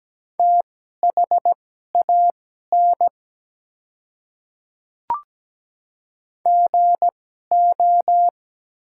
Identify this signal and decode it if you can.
Morse code, heard as a keyed tone, about 17 wpm, spaced as for 12 wpm, 700 Hz. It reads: THAN GO
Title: 2x Character Spacing - Top 200 Words - 17wpm